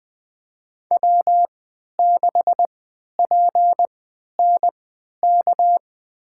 Morse code, heard as a keyed tone, about 20 wpm, spaced as for 11 wpm, 700 Hz. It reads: W6PNK
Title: USA Call Signs - 3x Character Spacing - 20wpm